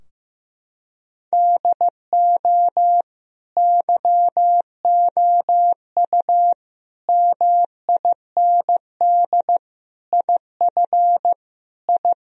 Morse code, heard as a keyed tone, about 16 wpm, 700 Hz. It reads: DO YOU MIND IF I
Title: Sentences from Top 1000 Words - Review Part 2 - 15wpm